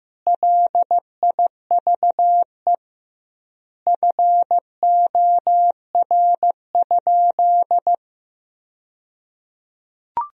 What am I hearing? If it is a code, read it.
Morse code, heard as a keyed tone, about 15 wpm, 700 Hz. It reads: LIVE FOR?
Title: 2x Word-Spacing - Sentences from Top 200 Words - 15wpm